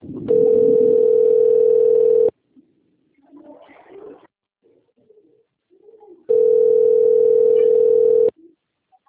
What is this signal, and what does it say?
Ringback tone (North America), 440 + 480 Hz.